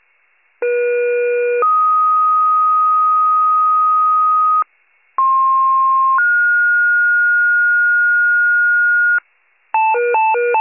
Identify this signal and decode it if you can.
Two-tone page, heard as a one-second tone, then a three-second tone, about 480 then 1250 Hz, about 1060 then 1430 Hz, with noise. It is heard from 0.6 to 9.2 s.
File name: pager / 98 FIRE / wcalert